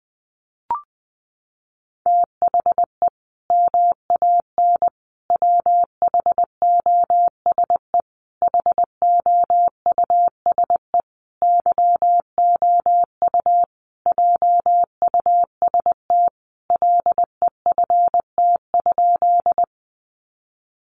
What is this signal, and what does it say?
Morse code, heard as a keyed tone, about 20 wpm, 700 Hz. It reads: THE MAN WHOSE HOUSE YOU JUST LEFT?
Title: Sentences from Top 1200 Words 20wpm